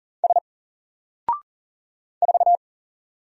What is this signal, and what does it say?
Morse code, heard as a keyed tone, about 40 wpm, 700 Hz. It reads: S 4
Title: Single Letter-Number 40wpm